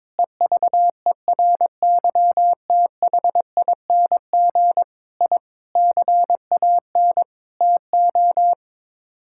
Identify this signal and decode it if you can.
Morse code, heard as a keyed tone, about 21 wpm, 700 Hz. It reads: EVERYTHING I CAN TO
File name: practice / 22wpm / Sets of 8 Words - 22wpm